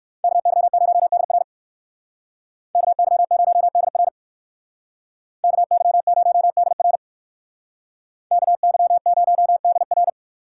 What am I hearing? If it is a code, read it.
Morse code, heard as a keyed tone, about 58 wpm, 700 Hz. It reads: KY0DR KY0DR KY0DR KY0DR